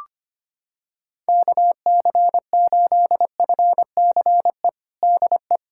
Morse code, heard as a keyed tone, about 25 wpm, 700 Hz. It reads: KC8FCE DE
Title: USA Callsign DE Callsign - 6 Characters - 25wpm